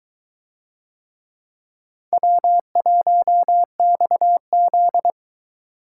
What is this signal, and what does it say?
Morse code, heard as a keyed tone, about 22 wpm, 700 Hz. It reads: W1XZ